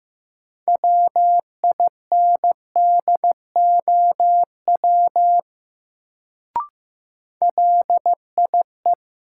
Morse code, heard as a keyed tone, about 15 wpm, 700 Hz. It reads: WINDOW LIE
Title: Sentences from Top 1600 Words - New Word Drill Alt - 15wpm